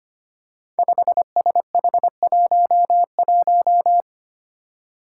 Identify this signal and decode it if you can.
Morse code, heard as a keyed tone, about 25 wpm, 700 Hz. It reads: 5SH11
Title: Five Challenging Letter-Number Combinations 25wpm